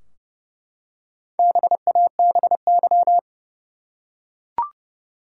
Morse code, heard as a keyed tone, about 32 wpm, 700 Hz. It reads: BABY
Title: Four Letter Words - 30wpm